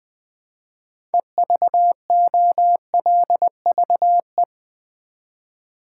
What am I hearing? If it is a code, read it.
Morse code, heard as a keyed tone, about 20 wpm, 700 Hz. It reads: EVOLVE
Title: 6 Letter Words - First 3 Letters Given - 20wpm